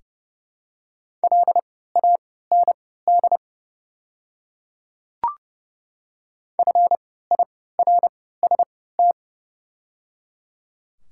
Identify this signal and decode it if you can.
Morse code, heard as a keyed tone, about 30 wpm, spaced as for 17 wpm, 700 Hz. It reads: LAND FIRST
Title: 3x Character Spacing - Top 200 Words - 30wpm